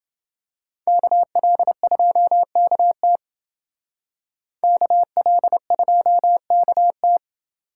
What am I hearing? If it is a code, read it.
Morse code, heard as a keyed tone, about 28 wpm, 700 Hz. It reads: KL2KT KL2KT